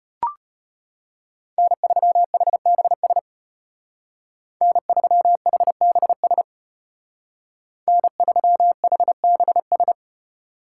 Morse code, heard as a keyed tone, about 33 wpm, 700 Hz. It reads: N3HBS N3HBS N3HBS